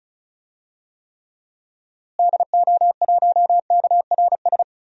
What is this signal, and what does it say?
Morse code, heard as a keyed tone, about 34 wpm, 700 Hz. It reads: DO1KRS
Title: DXCC-100 Call Signs - 6 Characters - Repeated 3x - 35wpm